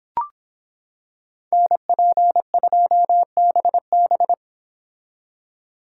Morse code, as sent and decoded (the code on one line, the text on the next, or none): -. .--. ..--- -... -...
NP2BB